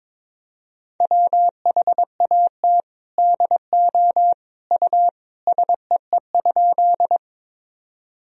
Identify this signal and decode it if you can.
Morse code, heard as a keyed tone, about 22 wpm, 700 Hz. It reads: WHAT DO U SEE?
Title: Sentences from Top 100 Words and Abbreviations 22wpm